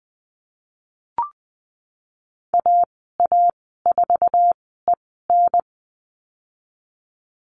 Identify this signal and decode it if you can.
Morse code, heard as a keyed tone, about 20 wpm, spaced as for 15 wpm, 700 Hz. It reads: AA4EN
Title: 076 Lesson 13g - Letter L - Call Signs 20wpm